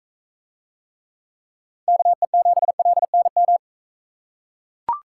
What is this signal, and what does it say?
Morse code, heard as a keyed tone, about 42 wpm, 700 Hz. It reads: KE7LNM